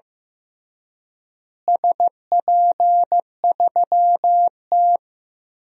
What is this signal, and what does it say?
Morse code, heard as a keyed tone, about 15 wpm, 700 Hz. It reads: SP3T
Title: DXCC-100 Call Signs - 4 Characters - Repeated 3x - 15wpm